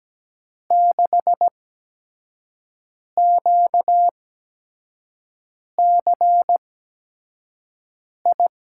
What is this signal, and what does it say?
Morse code, heard as a keyed tone, about 17 wpm, spaced as for 5 wpm, 700 Hz. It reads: 6QCI
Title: USA Call Signs - 5 Characters - 8x Character Spacing - 17wpm